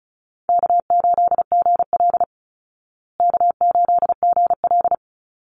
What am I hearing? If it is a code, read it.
Morse code, heard as a keyed tone, about 34 wpm, 700 Hz. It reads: K8GL K8GL
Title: Easy Call Signs Repeated 2x - 35wpm